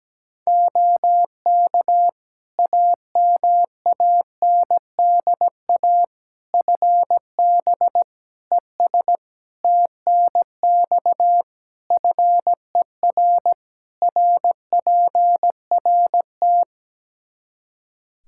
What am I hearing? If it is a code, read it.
Morse code, heard as a keyed tone, about 17 wpm, 700 Hz. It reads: OK AMANDA FB ES TNX FER RPRT
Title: K1USN SST QSO Practice 17wpm